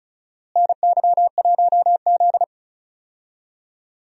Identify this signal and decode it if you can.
Morse code, heard as a keyed tone, about 34 wpm, 700 Hz. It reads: NY1Z
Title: USA Call Signs - 4 Characters - Repeated 3x - 35wpm